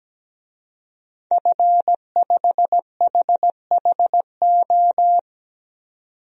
Morse code, heard as a keyed tone, about 17 wpm, 700 Hz. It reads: F5HHO
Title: DXCC-100 Call Signs - 5 Characters - 17wpm